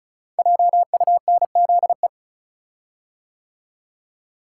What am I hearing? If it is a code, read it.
Morse code, heard as a keyed tone, about 36 wpm, 700 Hz. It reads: JUNZE
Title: Code Groups - 5 Letters - 35wpm